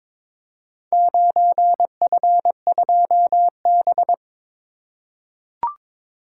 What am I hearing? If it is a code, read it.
Morse code, heard as a keyed tone, about 22 wpm, 700 Hz. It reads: 9F2B
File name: practice / 22wpm / Four Letter-Number Combinations - 22wpm